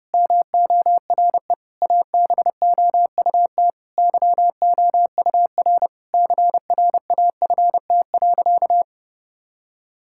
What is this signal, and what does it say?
Morse code, heard as a keyed tone, about 32 wpm, 700 Hz. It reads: MORE ABOUT YOUR CRAFT.
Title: Sentences from Top 1000 Words - Review Part 6 - 30wpm